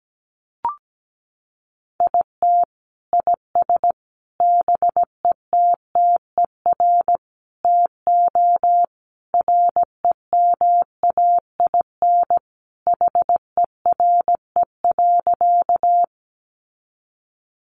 Morse code, heard as a keyed tone, about 17 wpm, 700 Hz. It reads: IT IS BETTER TO REMAIN HERE.
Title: Sentences from Top 1200 Words 17wpm